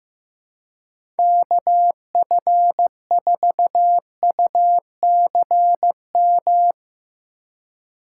Morse code, heard as a keyed tone, about 15 wpm, 700 Hz. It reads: KF4UCM
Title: USA Call Signs - 6 Characters - 3x Character Spacing - 15wpm